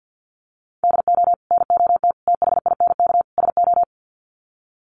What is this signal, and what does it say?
Morse code, heard as a keyed tone, about 50 wpm, 700 Hz. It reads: DO NOT THINK SO
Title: Sets of 4 Words 50wpm